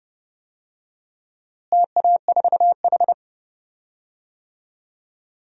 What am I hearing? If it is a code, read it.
Morse code, heard as a keyed tone, about 30 wpm, 700 Hz. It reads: TA4H
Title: DXCC-100 Call Signs - 4 Characters - Repeated 3x - 30wpm